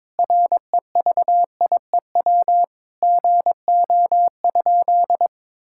Morse code, heard as a keyed tone, about 22 wpm, 700 Hz. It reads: REVIEW GO?
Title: Sentences from Top 1000 Words - Review Part 6 - 22wpm